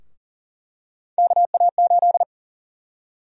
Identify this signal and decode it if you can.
Morse code, heard as a keyed tone, about 40 wpm, 700 Hz. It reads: KA8
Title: USA Call Sign Prefixes - Repeated 2x - 40wpm